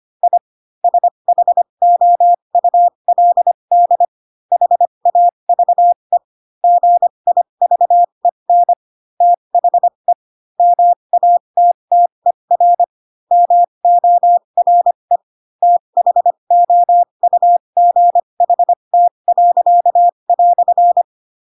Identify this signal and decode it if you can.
Morse code, heard as a keyed tone, about 25 wpm, spaced as for 22 wpm, 700 Hz. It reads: I SHOULD HAVE GIVEN THE MATTER MORE THOUGHT."
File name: Pinocchio / Pinocchio ch3 18 25 ik1hge